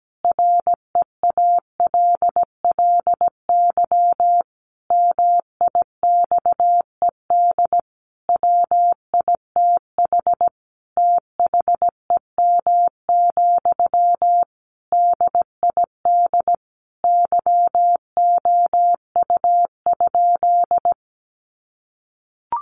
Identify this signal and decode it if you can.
Morse code, heard as a keyed tone, about 17 wpm, 700 Hz. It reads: REALLY MIXED WITH THEM, DID YOU?
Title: Sentences from Top 1500 Words 17wpm